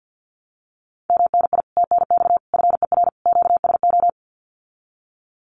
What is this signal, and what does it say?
Morse code, heard as a keyed tone, about 50 wpm, 700 Hz. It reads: MNI TNX FER QSO
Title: Common QSO Phrases Encore 50wpm